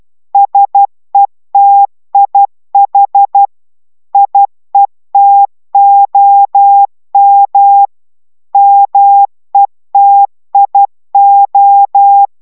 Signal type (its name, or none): Morse code